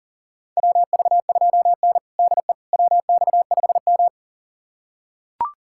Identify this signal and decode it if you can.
Morse code, heard as a keyed tone, about 40 wpm, 700 Hz. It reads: WV2N DE WX5M
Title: USA Callsign DE Callsign - 4 Characters - 40wpm